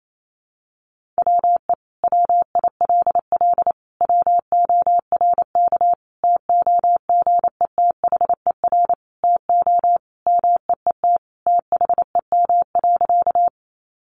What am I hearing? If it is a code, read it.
Morse code, heard as a keyed tone, about 28 wpm, 700 Hz. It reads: WE WILL WORK TOGETHER TO MEET THEM.
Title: Sentences from Top 1100 Words 28wpm